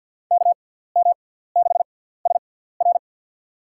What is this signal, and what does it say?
Morse code, heard as a keyed tone, about 52 wpm, spaced as for 17 wpm, 700 Hz. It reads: KM6SR
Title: USA Call Signs - 5 Characters - 6x Character Spacing - 50wpm